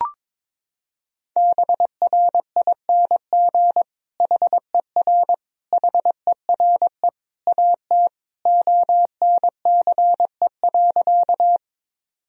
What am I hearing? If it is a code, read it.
Morse code, heard as a keyed tone, about 22 wpm, 700 Hz. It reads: BRING HER HERE AT ONCE.
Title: Sentences from Top 1000 Words - Review Part 3 - 22wpm